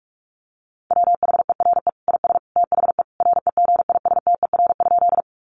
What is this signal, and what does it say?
Morse code, heard as a keyed tone, about 44 wpm, 700 Hz. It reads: WHERE IS THE REGISTER?
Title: Sentences from Top 1200 Words 45wpm